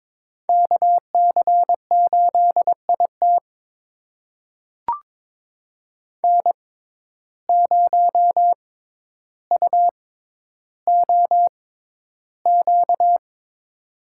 Morse code, heard as a keyed tone, about 22 wpm, spaced as for 11 wpm, 700 Hz. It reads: KC8IT N0UOQ
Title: USA Call Signs - 5 Characters - 6x Character Spacing - 22wpm